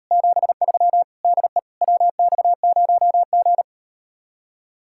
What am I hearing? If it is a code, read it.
Morse code, heard as a keyed tone, about 38 wpm, 700 Hz. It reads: 73 DE WX0G